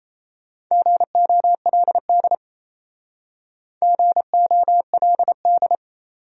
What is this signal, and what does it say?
Morse code, heard as a keyed tone, about 35 wpm, 700 Hz. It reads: GOLD GOLD